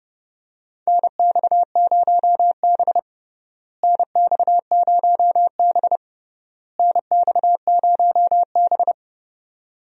Morse code, heard as a keyed tone, about 32 wpm, 700 Hz. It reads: NX0B NX0B NX0B
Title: USA Call Signs - 4 Characters - Repeated 3x - 30wpm